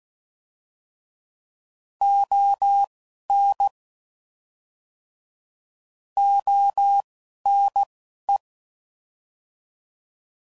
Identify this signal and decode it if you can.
Morse code, heard as a keyed tone, about 16 wpm, spaced as for 11 wpm, 800 Hz. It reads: ON ONE
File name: Prefixes / By First Letter / Os